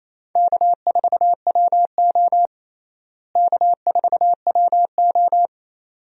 Morse code, heard as a keyed tone, about 28 wpm, 700 Hz. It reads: K4WO K4WO
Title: USA Call Signs - 4 Characters - Repeated 2x - 28wpm